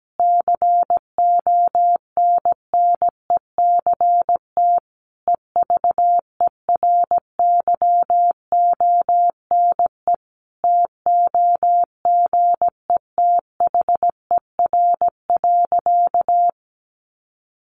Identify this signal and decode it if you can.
Morse code, heard as a keyed tone, about 17 wpm, 700 Hz. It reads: CONNECT EVERYONE TOGETHER.
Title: Sentences from Top 1500 Words 17wpm